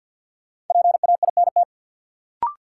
Morse code, heard as a keyed tone, about 47 wpm, 700 Hz. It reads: PAINT
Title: Sentences from Top 1000 Words - Review Part 6 - New Word Drill - 50wpm